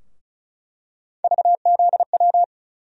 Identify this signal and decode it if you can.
Morse code, heard as a keyed tone, about 34 wpm, 700 Hz. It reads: VZW
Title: USA Call Sign Postfixes - 3 Letters 35wpm